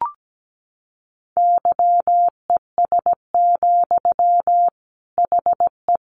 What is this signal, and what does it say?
Morse code, heard as a keyed tone, about 17 wpm, 700 Hz. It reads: YES, HE
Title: Sentences from Top 500 Words 17wpm